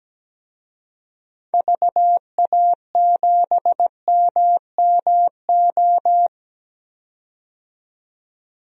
Morse code, heard as a keyed tone, about 17 wpm, 700 Hz. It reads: VA7MMO
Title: DXCC-100 Call Signs - 6 Characters - 17wpm